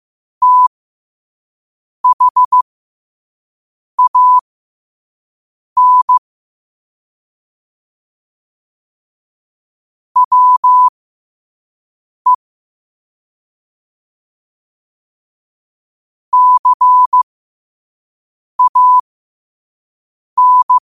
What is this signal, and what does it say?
Morse code, heard as a keyed tone, about 15 wpm, spaced as for 5 wpm, 1000 Hz. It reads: THAN WE CAN
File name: qotdfiles / 5wpm